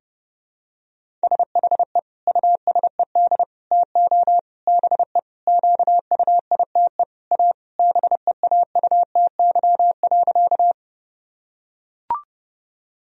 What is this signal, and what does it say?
Morse code, heard as a keyed tone, about 32 wpm, 700 Hz. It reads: SHE USED TO BE QUITE A BEAUTY.
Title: Sentences from Top 1000 Words - Review Part 2 - 30wpm